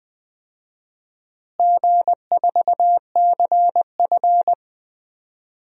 Morse code, heard as a keyed tone, about 21 wpm, 700 Hz. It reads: G4CF
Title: DXCC-100 Call Signs - 4 Characters - Repeated 3x - 20wpm